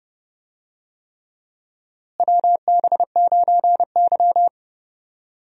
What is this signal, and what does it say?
Morse code, heard as a keyed tone, about 30 wpm, 700 Hz. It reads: WB9Y
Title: USA Call Signs - 4 Characters - Repeated 3x - 30wpm